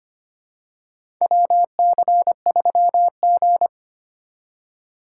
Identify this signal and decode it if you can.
Morse code, heard as a keyed tone, about 25 wpm, 700 Hz. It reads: WC3G
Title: USA Call Signs - 4 Characters - 2x Character Spacing - 25wpm